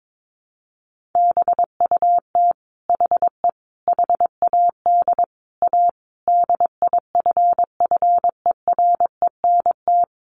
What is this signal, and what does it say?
Morse code, heard as a keyed tone, about 22 wpm, 700 Hz. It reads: BUT HE HAD A DIFFERENT
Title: Sentences from Top 1200 Words 22wpm